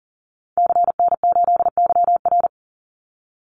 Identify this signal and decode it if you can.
Morse code, heard as a keyed tone, about 40 wpm, 700 Hz. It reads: CN8YR
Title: Call Signs Repeated 3x - 40wpm